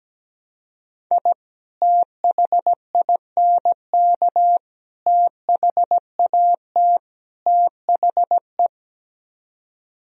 Morse code, heard as a keyed tone, about 17 wpm, 700 Hz. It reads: I THINK THAT THE
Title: Speed-Racing - Sets of 4 Words 17wpm